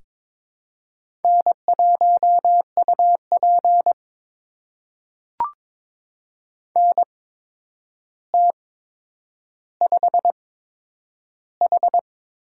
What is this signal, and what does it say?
Morse code, heard as a keyed tone, about 22 wpm, spaced as for 9 wpm, 700 Hz. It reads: N1UP NT5H
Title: USA Call Signs - 8x Character Spacing - 22wpm